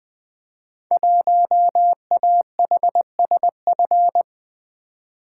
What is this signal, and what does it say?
Morse code, heard as a keyed tone, about 21 wpm, 700 Hz. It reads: 1AHSF
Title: Code Groups - 5 Letter:Numbers - 20wpm